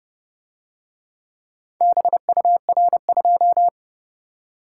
Morse code, heard as a keyed tone, about 29 wpm, 700 Hz. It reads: BUR2